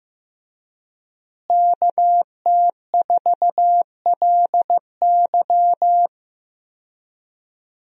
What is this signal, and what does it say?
Morse code, heard as a keyed tone, about 15 wpm, 700 Hz. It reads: KT4LY
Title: USA Call Signs - 5 Characters - Repeated 2x - 15wpm